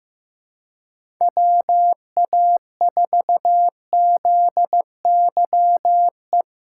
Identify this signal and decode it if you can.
Morse code, heard as a keyed tone, about 15 wpm, 700 Hz. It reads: WA4ZYE
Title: USA Call Signs - 6 Characters - 8x Character Spacing - 15wpm